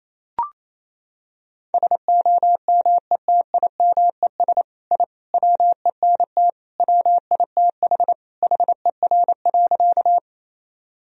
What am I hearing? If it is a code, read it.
Morse code, heard as a keyed tone, about 27 wpm, 700 Hz. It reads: SOMETIMES I WENT WITH HER.
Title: Sentences from Top 1000 Words - Review Part 3 - 28wpm